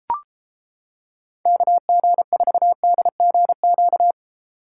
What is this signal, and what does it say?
Morse code, heard as a keyed tone, about 30 wpm, 700 Hz. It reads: KG4DGQ